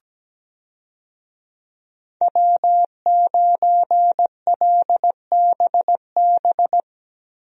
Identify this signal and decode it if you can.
Morse code, heard as a keyed tone, about 17 wpm, 700 Hz. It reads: W9LBB
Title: USA Call Signs - 5 Characters - Repeated 3x - 17wpm